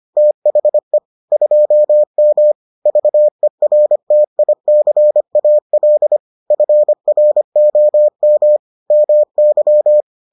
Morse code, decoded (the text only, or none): THE 2M VERTICAL FROM MY